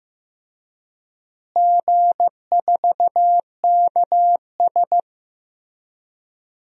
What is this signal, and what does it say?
Morse code, heard as a keyed tone, about 15 wpm, 700 Hz. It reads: G4KS
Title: DXCC-100 Call Signs - 4 Characters - Repeated 2x - 15wpm